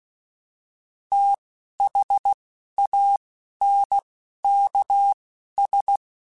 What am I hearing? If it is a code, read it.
Morse code, heard as a keyed tone, about 16 wpm, spaced as for 11 wpm, 800 Hz. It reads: THANKS